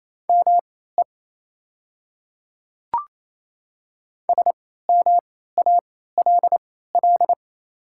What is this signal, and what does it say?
Morse code, heard as a keyed tone, about 28 wpm, spaced as for 16 wpm, 700 Hz. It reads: ME SMALL
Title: 3x Character Spacing - Top 300 Words - 28wpm